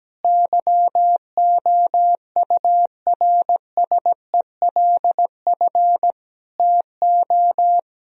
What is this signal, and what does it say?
Morse code, heard as a keyed tone, about 17 wpm, 700 Hz. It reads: YOURSELF TO
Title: Sentences from Top 1000 Words - Review Part 2 - 17wpm